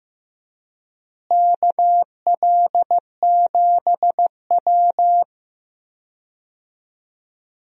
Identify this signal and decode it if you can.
Morse code, heard as a keyed tone, about 15 wpm, 700 Hz. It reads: KL7W
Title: DXCC-100 Call Signs - 4 Characters - Repeated 3x - 15wpm